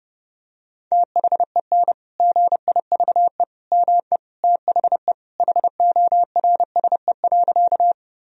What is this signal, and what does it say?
Morse code, heard as a keyed tone, about 30 wpm, 700 Hz. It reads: THEN GIVE ME THE HORSE.
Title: Sentences from Top 1000 Words - Review Part 1 - 30wpm